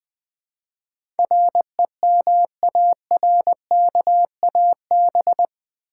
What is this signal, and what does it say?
Morse code, heard as a keyed tone, about 20 wpm, 700 Hz. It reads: REMARKAB
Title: Build A Word - 4 Letters At A Time - 10-Letter Words - 20wpm